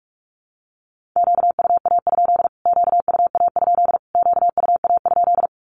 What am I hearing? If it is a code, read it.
Morse code, heard as a keyed tone, about 44 wpm, 700 Hz. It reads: QUA? QUA? QUA?
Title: Q-Codes Repeated 3x 45wpm